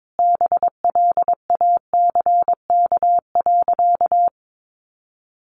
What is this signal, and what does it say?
Morse code, heard as a keyed tone, about 22 wpm, 700 Hz. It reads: BLACK.
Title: Sentences from Top 400 Words 22wpm